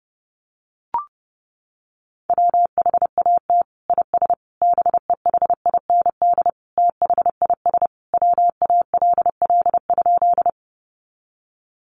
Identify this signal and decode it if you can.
Morse code, heard as a keyed tone, about 30 wpm, 700 Hz. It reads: WHAT IS BEHIND THIS WALL?
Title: Sentences from Top 1100 Words 30wpm